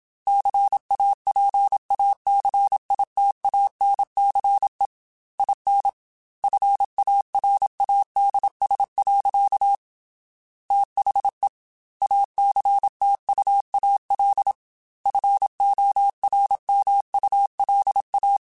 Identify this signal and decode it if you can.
Morse code, heard as a keyed tone, about 27 wpm, 800 Hz. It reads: CAPACITANCE IN FARADS. THE ACTUAL FORMULA
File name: Technical Subjects / Technical Subjects-mp3s / XC @25